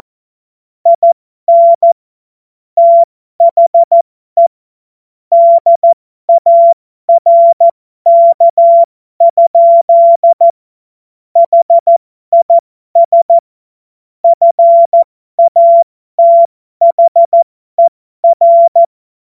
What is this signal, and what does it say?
Morse code, heard as a keyed tone, about 14 wpm, spaced as for 12 wpm, 680 Hz. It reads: IN THE DARK? HIS FATHER
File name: Morsekurs / Witze